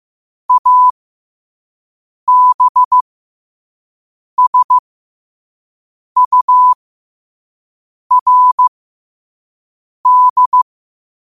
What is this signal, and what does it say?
Morse code, heard as a keyed tone, about 15 wpm, spaced as for 5 wpm, 1000 Hz. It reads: ABSURD
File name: qotdfiles / 5wpm